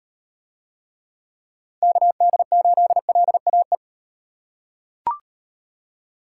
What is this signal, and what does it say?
Morse code, heard as a keyed tone, about 36 wpm, 700 Hz. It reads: KD8LAE